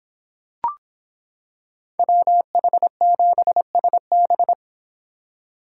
Morse code, heard as a keyed tone, about 26 wpm, 700 Hz. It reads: WH7SB